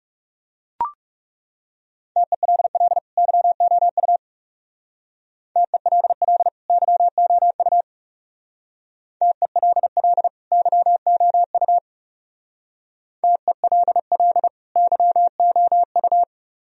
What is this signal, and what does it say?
Morse code, heard as a keyed tone, about 35 wpm, 700 Hz. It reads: TELL YOU TELL YOU TELL YOU TELL YOU
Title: Speed-Racing - Sets of 2 Words 30wpm